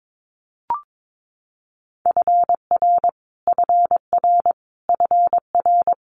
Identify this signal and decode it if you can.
Morse code, heard as a keyed tone, about 22 wpm, 700 Hz. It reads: FR FR FR
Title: Multi-letter Phonemes Repeated 3x 22wpm